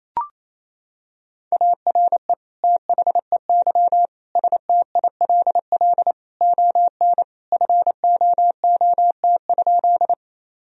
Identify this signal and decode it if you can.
Morse code, heard as a keyed tone, about 28 wpm, 700 Hz. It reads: ARE THEY STILL ON FOOT?
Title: Sentences from Top 1000 Words - Review Part 3 - 28wpm